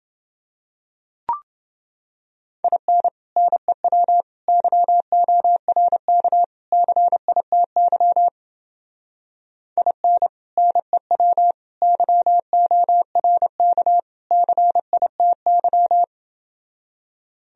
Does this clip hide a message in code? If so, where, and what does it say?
Yes: Morse code from 2.6 to 16.0 s, heard as a keyed tone, about 30 wpm, 700 Hz. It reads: IN NEW YORK CITY IN NEW YORK CITY